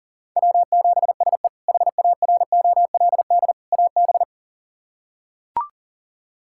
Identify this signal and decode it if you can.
Morse code, heard as a keyed tone, about 40 wpm, 700 Hz. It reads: W7SE HAROLD AB
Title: K1USN SST QSO Practice - 40wpm